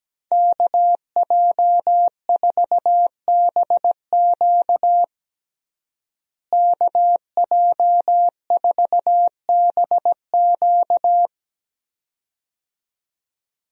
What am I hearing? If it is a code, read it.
Morse code, heard as a keyed tone, about 17 wpm, 700 Hz. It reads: KJ4BQ KJ4BQ
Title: USA Call Signs - 5 Characters - Repeated 2x - 17wpm